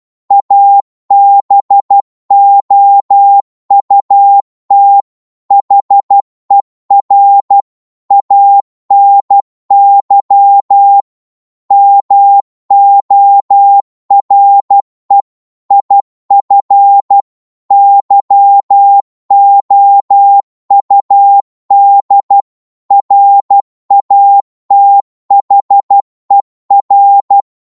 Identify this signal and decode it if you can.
Morse code, heard as a keyed tone, about 12 wpm, 800 Hz. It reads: ABOUT HER ANY MORE IF YOUD RATHER
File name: Morse / psg-12wpm